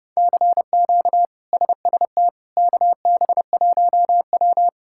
Morse code, heard as a keyed tone, about 30 wpm, 700 Hz. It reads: CQ SST KB1W
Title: K1USN SST QSO Practice - 30wpm